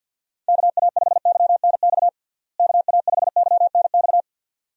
Morse code, heard as a keyed tone, about 47 wpm, 700 Hz. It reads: KA5YNX KA5YNX
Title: USA Call Signs - 6 Characters - Repeated 2x - 50wpm